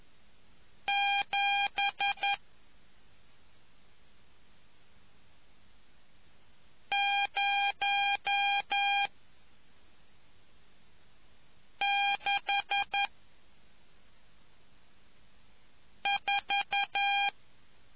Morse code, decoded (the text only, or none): 7 0 6 4